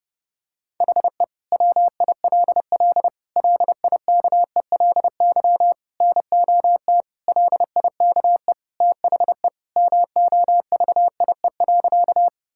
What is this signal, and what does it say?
Morse code, heard as a keyed tone, about 32 wpm, 700 Hz. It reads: HE WILL LIKELY NOT LIKE THE MOVIE.
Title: Sentences from Top 1000 Words - Review Part 5 - 30wpm